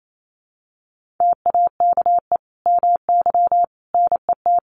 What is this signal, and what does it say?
Morse code, heard as a keyed tone, about 28 wpm, 700 Hz. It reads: TAKE MY NET
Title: Sentences from Top 1400 Words 28wpm